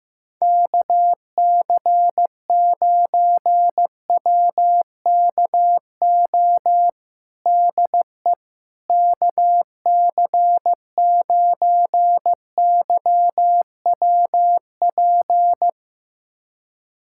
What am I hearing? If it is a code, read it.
Morse code, heard as a keyed tone, about 15 wpm, 700 Hz. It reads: KC9WKO DE KC9YWP
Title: USA Callsign DE Callsign 15wpm